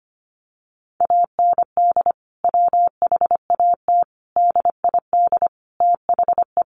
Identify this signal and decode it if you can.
Morse code, heard as a keyed tone, about 25 wpm, 700 Hz. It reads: AND WHAT DID THE
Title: Sentences from Top 1500 Words 25wpm